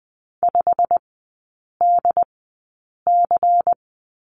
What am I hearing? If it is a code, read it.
Morse code, heard as a keyed tone, about 20 wpm, spaced as for 8 wpm, 700 Hz. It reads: 5DC
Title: 191 Lesson 31e - Letter v - Three Characters 20wpm